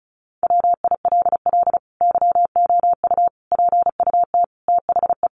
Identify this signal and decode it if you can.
Morse code, heard as a keyed tone, about 34 wpm, 700 Hz. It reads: WILL YOU PUT THE
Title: Sentences from Top 1400 Words 35wpm